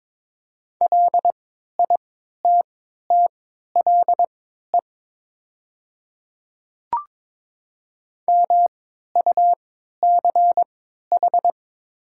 Morse code, heard as a keyed tone, about 22 wpm, spaced as for 12 wpm, 700 Hz. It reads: LITTLE MUCH
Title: 3x Character Spacing - Top 200 Words - 22wpm